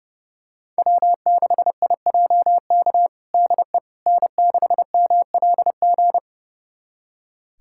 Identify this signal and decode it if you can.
Morse code, heard as a keyed tone, about 30 wpm, 700 Hz. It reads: W6IJK DE N6MLG
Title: USA Callsign DE Callsign - 5 Characters - 30wpm